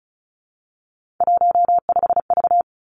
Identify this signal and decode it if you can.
Morse code, heard as a keyed tone, about 36 wpm, 700 Hz. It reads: 15V
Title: Three Challenging Letter-Number Combinations 35wpm